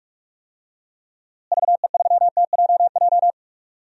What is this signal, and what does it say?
Morse code, heard as a keyed tone, about 44 wpm, 700 Hz. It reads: VE3TJJ
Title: DXCC-100 Call Signs - 6 Characters - Repeated 3x - 45wpm